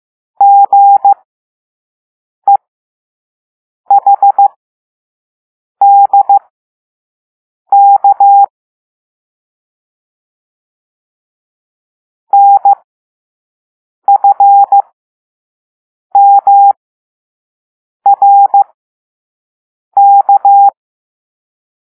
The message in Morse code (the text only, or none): GEHDK NFMRK